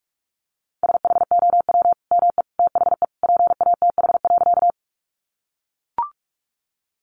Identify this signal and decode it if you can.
Morse code, heard as a keyed tone, about 44 wpm, 700 Hz. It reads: SHOW ME THE PATH.
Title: Sentences from Top 1400 Words 45wpm